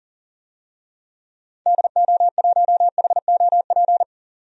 Morse code, heard as a keyed tone, about 40 wpm, 700 Hz. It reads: DO1HOP
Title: DXCC-100 Call Signs - 6 Characters - 40wpm